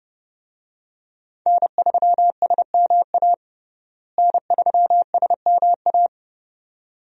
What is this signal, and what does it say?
Morse code, heard as a keyed tone, about 32 wpm, 700 Hz. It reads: N3SMA N3SMA
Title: USA Call Signs - 5 Characters - Repeated 2x - 30wpm